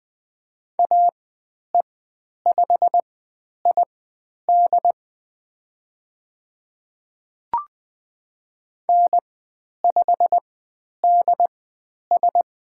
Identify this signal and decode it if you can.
Morse code, heard as a keyed tone, about 20 wpm, spaced as for 10 wpm, 700 Hz. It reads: AE5ID N5DS
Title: 109 Lesson 18g - Letter m - Call Signs 20wpm